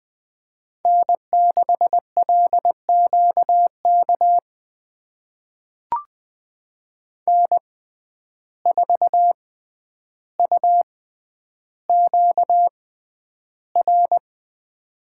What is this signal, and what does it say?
Morse code, heard as a keyed tone, about 20 wpm, spaced as for 10 wpm, 700 Hz. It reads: N6LQK N4UQR
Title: USA Call Signs - 5 Characters - 6x Character Spacing - 20wpm